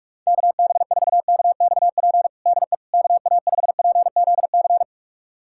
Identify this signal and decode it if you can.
Morse code, heard as a keyed tone, about 46 wpm, 700 Hz. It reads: KB4KXP DE KA5PZC
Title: USA Callsign DE Callsign - 6 Characters - 45wpm